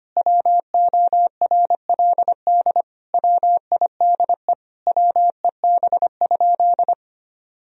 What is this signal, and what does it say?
Morse code, heard as a keyed tone, about 25 wpm, 700 Hz. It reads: WORLD WIDE WEB?
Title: Sentences from Top 1000 Words - Review Part 4 - 25wpm